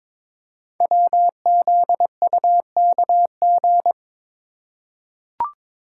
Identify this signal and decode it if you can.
Morse code, heard as a keyed tone, about 22 wpm, 700 Hz. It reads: WZUKG
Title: Code Groups - 5 Letters - 22wpm